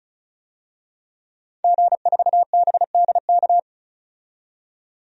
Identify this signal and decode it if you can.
Morse code, heard as a keyed tone, about 36 wpm, 700 Hz. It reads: G4BDK
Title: DXCC-100 Call Signs - 5 Characters - 35wpm